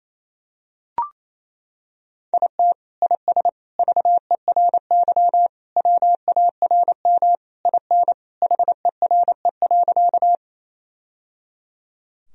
Morse code, heard as a keyed tone, about 27 wpm, 700 Hz. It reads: IT IS VERY WARM IN HERE.
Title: Sentences from Top 1000 Words - Review Part 2 - 28wpm